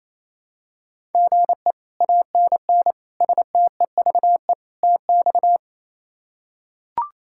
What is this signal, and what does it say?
Morse code, heard as a keyed tone, about 27 wpm, 700 Hz. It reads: GE ANN STEVE TX
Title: K1USN SST QSO Practice - 28wpm